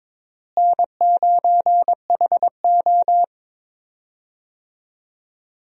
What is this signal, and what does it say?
Morse code, heard as a keyed tone, about 22 wpm, 700 Hz. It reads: N9HO
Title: USA Call Signs - 4 Characters - Repeated 2x - 22wpm